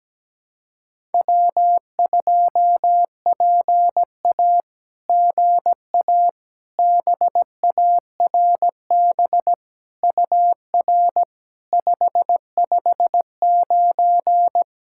Morse code, heard as a keyed tone, about 17 wpm, 700 Hz. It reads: W2PA GA BARB UR 559
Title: SOTA QSO Practice 17wpm